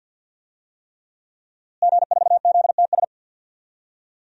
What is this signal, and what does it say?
Morse code, heard as a keyed tone, about 52 wpm, 700 Hz. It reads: G4ZTS
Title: DXCC-100 Call Signs - 5 Characters - Repeated 3x - 50wpm